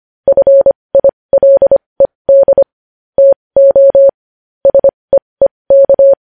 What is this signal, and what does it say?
Morse code, heard as a keyed tone, about 25 wpm, spaced as for 20 wpm, 550 Hz. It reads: FILED TO SEEK
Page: news headlines in morse code at 20 wpm